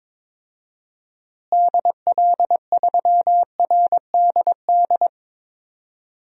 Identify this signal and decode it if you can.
Morse code, heard as a keyed tone, about 22 wpm, 700 Hz. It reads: DL3RDD
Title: DXCC-100 Call Signs - 6 Characters - Repeated 3x - 22wpm